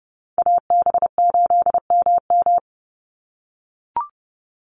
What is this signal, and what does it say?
Morse code, heard as a keyed tone, about 30 wpm, 700 Hz. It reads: AB8MM
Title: USA Call Signs 30wpm